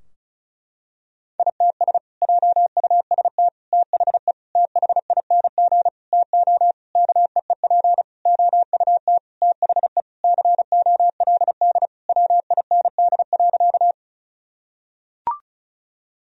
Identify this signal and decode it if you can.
Morse code, heard as a keyed tone, about 36 wpm, 700 Hz. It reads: ITS JUST THE THING TO KEEP OUT THE COLD WIND.
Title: Sentences from Top 1000 Words - Review Part 2 - 35wpm